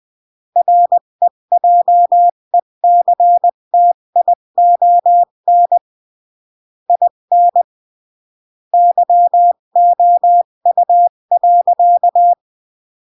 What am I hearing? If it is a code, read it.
Morse code, heard as a keyed tone, about 20 wpm, 700 Hz. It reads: REJECTION IN YOU.
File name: CW / jodorowsky / 82 commandments 20-15